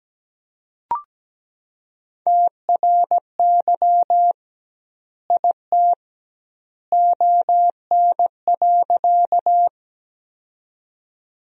Morse code, heard as a keyed tone, about 17 wpm, 700 Hz. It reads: TRY IT ON.
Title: 2x Word-Spacing - Sentences from Top 200 Words - 17wpm